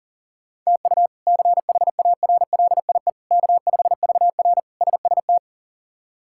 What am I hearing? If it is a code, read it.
Morse code, heard as a keyed tone, about 40 wpm, 700 Hz. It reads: TU CHARLIE K5VR SST
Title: K1USN SST QSO Practice - 40wpm